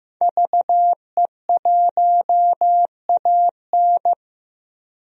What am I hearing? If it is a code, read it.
Morse code, heard as a keyed tone, about 15 wpm, 700 Hz. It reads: VE1AN
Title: DXCC-100 Call Signs - 5 Characters - Repeated 2x - 15wpm